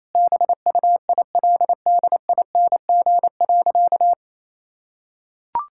Morse code, heard as a keyed tone, about 28 wpm, 700 Hz. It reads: BUILDING.
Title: Sentences from Top 1000 Words - Review Part 5 - 28wpm